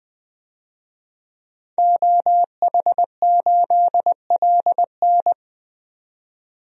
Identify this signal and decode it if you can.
Morse code, heard as a keyed tone, about 20 wpm, 700 Hz. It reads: OH8LN
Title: DXCC-100 Call Signs - 5 Characters - 20wpm